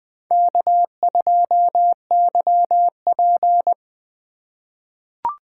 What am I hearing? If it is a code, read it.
Morse code, heard as a keyed tone, about 21 wpm, 700 Hz. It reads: K2YP
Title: USA Call Signs - 4x Character Spacing - 20wpm